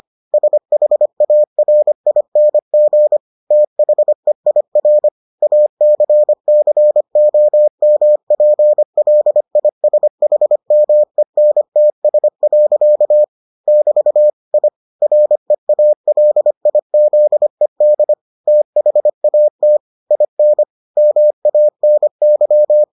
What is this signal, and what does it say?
Morse code, heard as a keyed tone, about 25 wpm, 600 Hz. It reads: SHARING THEIR ACCOMPLISHMENTS. <BT> I REALIZED THAT IN MANY